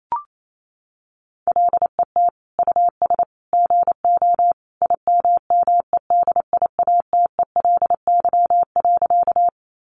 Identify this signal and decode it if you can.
Morse code, heard as a keyed tone, about 28 wpm, 700 Hz. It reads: LET US GO IMMEDIATELY.
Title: Sentences from Top 1100 Words 28wpm